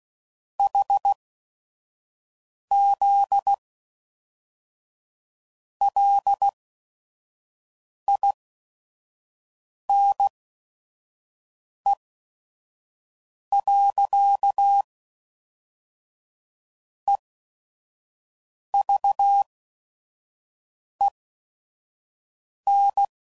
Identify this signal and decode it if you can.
Morse code, heard as a keyed tone, about 16 wpm, spaced as for 5 wpm, 800 Hz. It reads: HZ LINE. EVEN
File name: CW Code Groups @5WPM